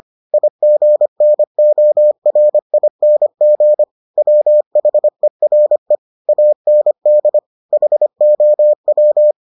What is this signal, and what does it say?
Morse code, heard as a keyed tone, about 25 wpm, 600 Hz. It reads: IGNORING WHERE AND HOW